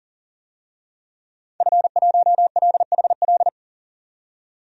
Morse code, heard as a keyed tone, about 37 wpm, 700 Hz. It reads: F1LHL
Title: DXCC-100 Call Signs - 5 Characters - 40wpm